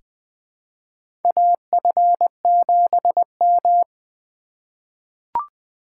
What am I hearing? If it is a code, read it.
Morse code, heard as a keyed tone, about 21 wpm, 700 Hz. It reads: AF7M
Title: 2023 SOTA Chaser Call Signs - 20wpm